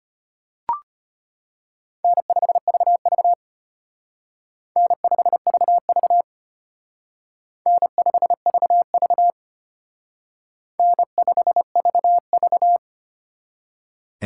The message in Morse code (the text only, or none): N5VV N5VV N5VV N5VV